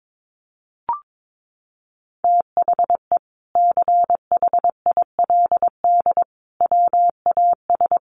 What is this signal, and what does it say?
Morse code, heard as a keyed tone, about 22 wpm, 700 Hz. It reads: THE CHILD WAS
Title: Sentences from Top 1400 Words 22wpm